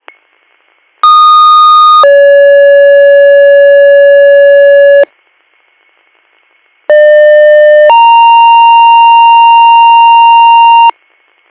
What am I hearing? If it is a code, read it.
Two-tone page, heard as a one-second tone, then a three-second tone, about 1230 then 560 Hz, about 600 then 930 Hz. It is heard from 1.0 to 10.9 s.